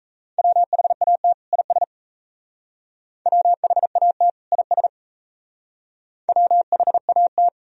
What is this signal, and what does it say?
Morse code, heard as a keyed tone, about 35 wpm, 700 Hz. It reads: WHAT IS WHAT IS WHAT
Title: Speed-Racing - Sets of 2 Words 28wpm